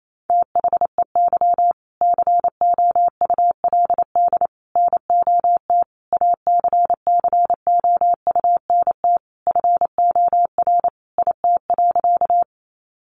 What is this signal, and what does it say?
Morse code, heard as a keyed tone, about 28 wpm, 700 Hz. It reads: THEY COULD NOT ACCOUNT FOR IT.
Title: Sentences from Top 900 Words 28wpm